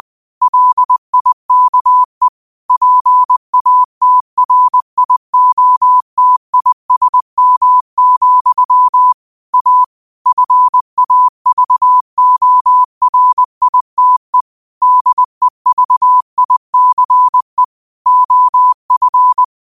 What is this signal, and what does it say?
Morse code, heard as a keyed tone, about 20 wpm, 1000 Hz. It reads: LIKE PATRIOTISM, A FAVORITE DEVICE OF
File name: qotdfiles / 20wpm